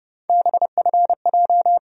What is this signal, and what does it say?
Morse code, heard as a keyed tone, about 32 wpm, 700 Hz. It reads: BFJ
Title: USA Call Sign Postfixes - 3 Letters 30wpm